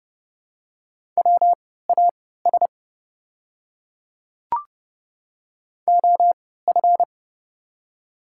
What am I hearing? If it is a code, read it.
Morse code, heard as a keyed tone, about 30 wpm, spaced as for 17 wpm, 700 Hz. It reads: WAS OF
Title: 3x Character Spacing - Top 200 Words - 30wpm